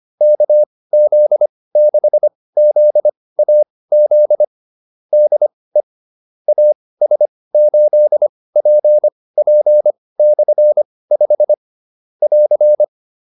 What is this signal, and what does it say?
Morse code, heard as a keyed tone, about 25 wpm, spaced as for 18 wpm, 600 Hz. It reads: KZ6ZAZ DE AS8PP/5 <AR>
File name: QSO302 18